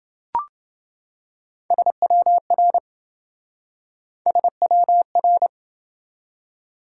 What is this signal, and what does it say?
Morse code, heard as a keyed tone, about 28 wpm, 700 Hz. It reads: SWR SWR